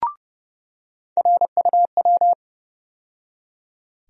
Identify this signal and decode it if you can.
Morse code, heard as a keyed tone, about 31 wpm, 700 Hz. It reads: RUW
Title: USA Call Sign Postfixes - 3 Letters 30wpm